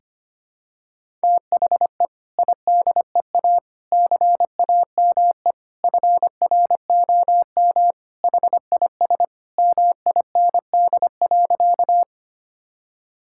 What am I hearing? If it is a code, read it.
Morse code, heard as a keyed tone, about 25 wpm, 700 Hz. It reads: THE IDEA CAME FROM HIS MIND.
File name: practice / 25wpm / Sentences with 2-Word Phrases - 25wpm